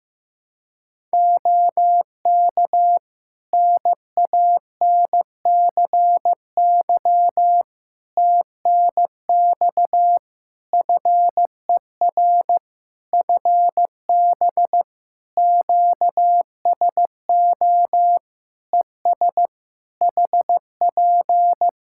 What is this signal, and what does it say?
Morse code, heard as a keyed tone, about 15 wpm, 700 Hz. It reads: OK NANCY TNX FER FB QSO ES HP
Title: Practice QSO - Exchange 4 - Ending - 15wpm